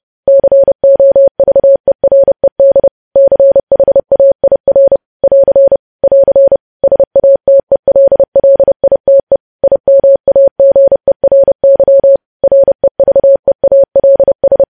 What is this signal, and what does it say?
Morse code, heard as a keyed tone, about 30 wpm, 550 Hz. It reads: COVERED CHAIR <AR> <AR> SATELLITE IMAGERY REVEALS
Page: news headlines in morse code at 30 wpm